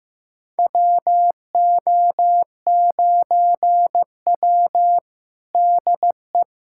Morse code, heard as a keyed tone, about 15 wpm, 700 Hz. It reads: WO9W DE
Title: USA Callsign DE Callsign - 4 Characters - 15wpm